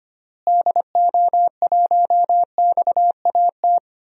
Morse code, heard as a keyed tone, about 25 wpm, 700 Hz. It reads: DO1XAT
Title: DXCC-100 Call Signs - 6 Characters - Repeated 2x - 25wpm